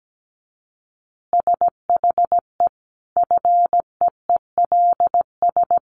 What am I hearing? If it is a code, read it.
Morse code, heard as a keyed tone, about 17 wpm, 700 Hz. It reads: SHE FEELS
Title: Additional Easy Phrases 17wpm